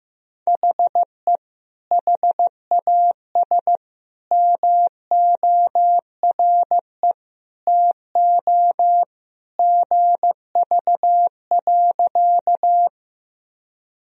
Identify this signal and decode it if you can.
Morse code, heard as a keyed tone, about 15 wpm, 700 Hz. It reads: HE HAS MORE TO GV.
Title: Sentences from Top 200 Words and Abbreviations 15wpm